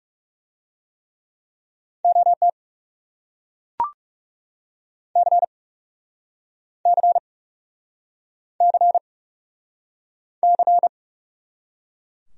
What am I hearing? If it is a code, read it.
Morse code, heard as a keyed tone, about 40 wpm, 700 Hz. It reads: OT C C C C